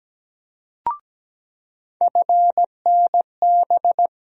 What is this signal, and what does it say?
Morse code, heard as a keyed tone, about 17 wpm, 700 Hz. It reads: FNB
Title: USA Call Sign Postfixes - 3 Letters 17wpm